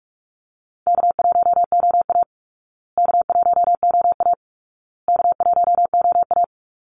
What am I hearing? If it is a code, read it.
Morse code, heard as a keyed tone, about 44 wpm, 700 Hz. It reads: K1OA K1OA K1OA
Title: Call Signs Repeated 2x - 45wpm